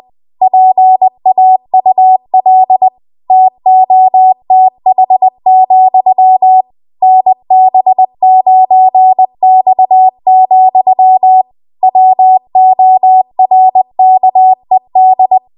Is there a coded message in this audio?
Yes: Morse code from 0.4 to 15.4 s, heard as a keyed tone, about 20 wpm, 750 Hz. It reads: PAUL TOTH, NB9X, WORKED